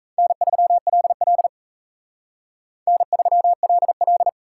Morse code, heard as a keyed tone, about 35 wpm, 700 Hz. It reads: N3LL N3LL
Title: Speed Racing - USA 1x2 Call Signs 28wpm